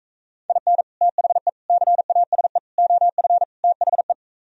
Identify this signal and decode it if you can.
Morse code, heard as a keyed tone, about 46 wpm, 700 Hz. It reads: IN THE CASE OF THE